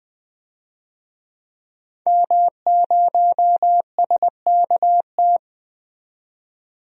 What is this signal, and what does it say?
Morse code, heard as a keyed tone, about 20 wpm, 700 Hz. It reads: M0SKT
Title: DXCC-100 Call Signs - 5 Characters - 20wpm